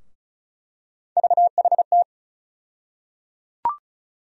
Morse code, heard as a keyed tone, about 34 wpm, 700 Hz. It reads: VHT
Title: USA Call Sign Postfixes - 3 Letters 35wpm